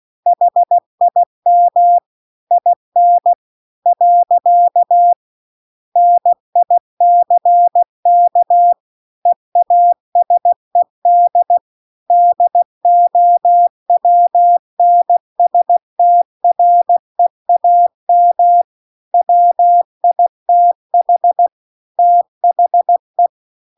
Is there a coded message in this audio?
Yes: Morse code from 0.3 to 23.3 s, heard as a keyed tone, about 16 wpm, 700 Hz. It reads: HIM IN . NICK EASED DOWNSTREAM WITH THE